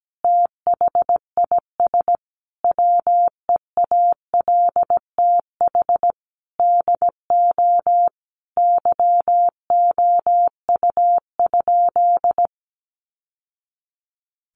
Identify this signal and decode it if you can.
Morse code, heard as a keyed tone, about 17 wpm, 700 Hz. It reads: THIS WEALTH DO YOU?
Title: Sentences from Top 1200 Words 17wpm